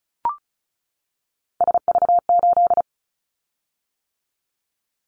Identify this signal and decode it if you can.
Morse code, heard as a keyed tone, about 34 wpm, 700 Hz. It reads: SV8
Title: Callsign Prefixes for Top 50 DXCC - 35wpm